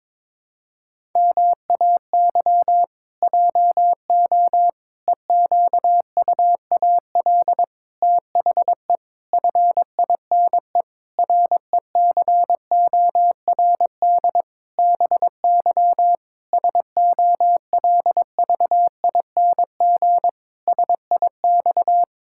Morse code, heard as a keyed tone, about 22 wpm, 700 Hz. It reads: MAY JO EQUAL THE FINE RECORD BY SOLVING SIX
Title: Pangram Challenge v2 - 22wpm